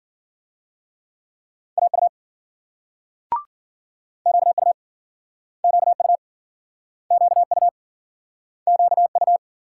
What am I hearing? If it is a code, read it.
Morse code, heard as a keyed tone, about 57 wpm, 700 Hz. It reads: AU QU QU QU QU